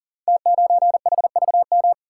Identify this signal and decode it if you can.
Morse code, heard as a keyed tone, about 39 wpm, 700 Hz. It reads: T9HVM
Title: Code Groups - 5 Letter:Numbers - 40wpm